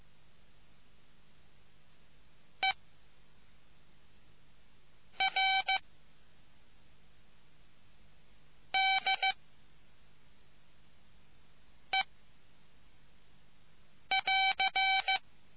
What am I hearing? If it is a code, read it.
Morse code, heard as a keyed tone, about 15 wpm, 770 Hz. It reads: E R D E <AR>